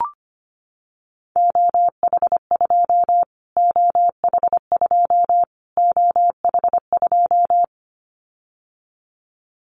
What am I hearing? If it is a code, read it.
Morse code, heard as a keyed tone, about 25 wpm, 700 Hz. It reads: OH2 OH2 OH2
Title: Callsign Prefixes for Top 25 DXCC - Repeated 3x - 25wpm